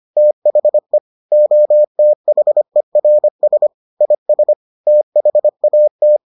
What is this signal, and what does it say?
Morse code, heard as a keyed tone, about 25 wpm, 600 Hz. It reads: THE OTHERS IS THAT